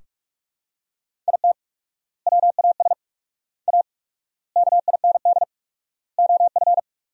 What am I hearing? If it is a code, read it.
Morse code, heard as a keyed tone, about 47 wpm, 700 Hz. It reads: IT WAS A KIND OF
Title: 4x Word-Spacing - Sentences from Top 500 Words - 45wpm